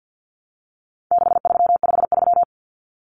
Morse code, heard as a keyed tone, about 50 wpm, 700 Hz. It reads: 6353
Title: Numbers - 4-Digit 50wpm